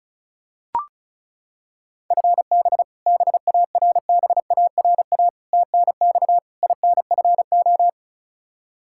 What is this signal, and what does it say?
Morse code, heard as a keyed tone, about 36 wpm, 700 Hz. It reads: FB BARBARA TNX INFO
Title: Practice QSO - Exchange 3 - More Info - 35wpm